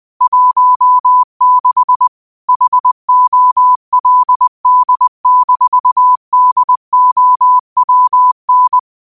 Morse code, heard as a keyed tone, about 20 wpm, 1000 Hz. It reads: 16 HOLD-DOWN